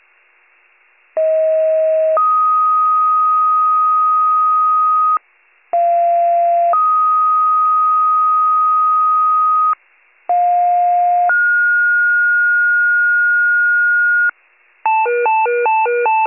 Two-tone page, heard as a one-second tone, then a three-second tone, about 640 then 1250 Hz, about 690 then 1250 Hz, about 710 then 1430 Hz. It is heard from 1.2 to 14.3 s.